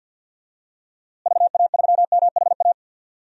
Morse code, heard as a keyed tone, about 52 wpm, 700 Hz. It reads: VA3MHA
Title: DXCC-100 Call Signs - 6 Characters - Repeated 2x - 50wpm